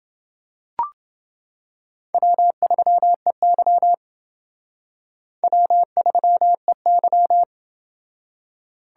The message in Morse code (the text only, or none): W3EY W3EY